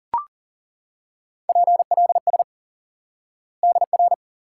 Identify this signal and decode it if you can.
Morse code, heard as a keyed tone, about 39 wpm, 700 Hz. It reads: PLS DR
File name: IQE Course 3 - Lesson 89 - YAGI - Two Elements - 40wpm